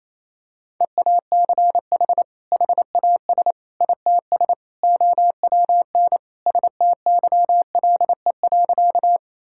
Morse code, heard as a keyed tone, about 28 wpm, 700 Hz. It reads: EACH HAS ITS OWN STYLE.
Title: Sentences from Top 1000 Words - Review Part 5 - 28wpm